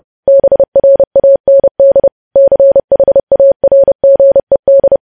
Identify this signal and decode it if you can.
Morse code, heard as a keyed tone, about 30 wpm, 550 Hz. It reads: BRAND CHARGED
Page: news headlines in morse code at 30 wpm